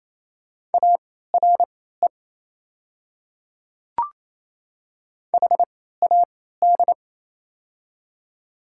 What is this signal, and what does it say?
Morse code, heard as a keyed tone, about 28 wpm, spaced as for 16 wpm, 700 Hz. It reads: ARE HAD
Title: 3x Character Spacing - Top 100 Words - 28wpm